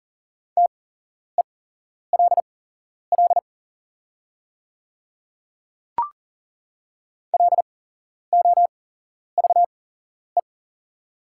Morse code, heard as a keyed tone, about 40 wpm, spaced as for 11 wpm, 690 Hz. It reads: TELL LOVE
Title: 8x Character Spacing - Top 500 Words - 40wpm